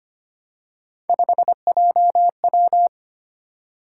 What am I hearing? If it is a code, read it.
Morse code, heard as a keyed tone, about 25 wpm, 700 Hz. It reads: 5JW
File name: Three Letter-Number Combinations - 25wpm